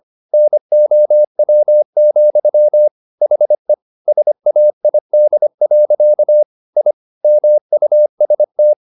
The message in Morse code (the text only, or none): NOW, HE SAID. I MUST